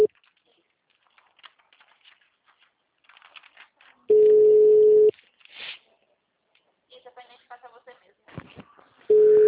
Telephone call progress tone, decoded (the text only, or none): ringback tone